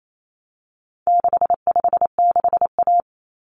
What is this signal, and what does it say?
Morse code, heard as a keyed tone, about 28 wpm, 700 Hz. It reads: 656A
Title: Common Cut Numbers - 4-Digit 28wpm